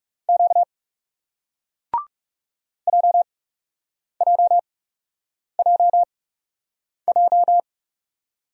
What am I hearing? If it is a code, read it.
Morse code, heard as a keyed tone, about 37 wpm, spaced as for 8 wpm, 700 Hz. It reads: Q JJJJ
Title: Speed-Racing Single Letter-Number 30wpm